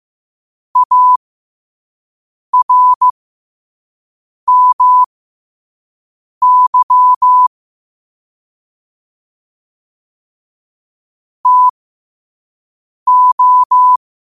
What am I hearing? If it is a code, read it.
Morse code, heard as a keyed tone, about 15 wpm, spaced as for 5 wpm, 1000 Hz. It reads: ARMY TO
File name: qotdfiles / 5wpm